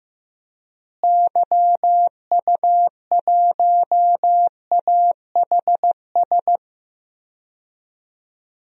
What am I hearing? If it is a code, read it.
Morse code, heard as a keyed tone, about 15 wpm, 700 Hz. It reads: YU1AHS
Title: DXCC-100 Call Signs - 6 Characters - Repeated 3x - 15wpm